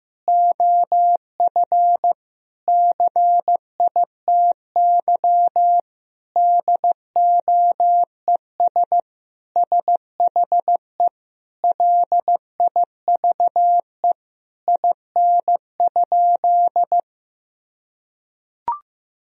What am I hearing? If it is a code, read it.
Morse code, heard as a keyed tone, about 15 wpm, 700 Hz. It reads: OF CITY DOES SHE LIVE IN?
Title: Sentences from Top 1000 Words - Review Part 2 - 15wpm